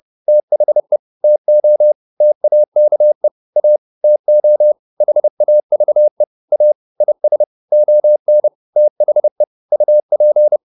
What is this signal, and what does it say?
Morse code, heard as a keyed tone, about 30 wpm, 600 Hz. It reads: THE TO TAKE A TO HAVE A IS ON THE UP